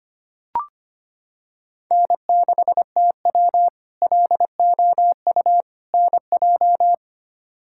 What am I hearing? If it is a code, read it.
Morse code, heard as a keyed tone, about 25 wpm, 700 Hz. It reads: N6TW LOU NJ
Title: K1USN SST QSO Practice - 25wpm